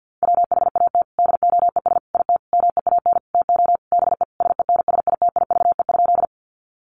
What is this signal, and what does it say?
Morse code, heard as a keyed tone, about 50 wpm, 700 Hz. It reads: WHAT DOES IT MEAN TO BE SENSITIVE?
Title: Sentences from Top 1400 Words 50wpm